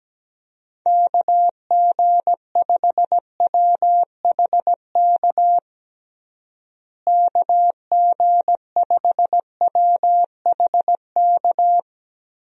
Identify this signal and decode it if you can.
Morse code, heard as a keyed tone, about 17 wpm, 700 Hz. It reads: KG5WHK KG5WHK
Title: USA Call Signs - 6 Characters - Repeated 2x - 17wpm